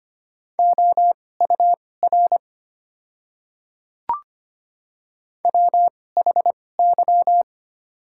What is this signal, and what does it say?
Morse code, heard as a keyed tone, about 25 wpm, spaced as for 18 wpm, 700 Hz. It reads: OUR WHY
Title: 2x Character Spacing - Top 200 Words - 25wpm